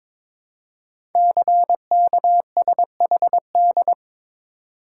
Morse code, heard as a keyed tone, about 22 wpm, 700 Hz. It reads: CKSHD
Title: Code Groups - 5 Letters - 22wpm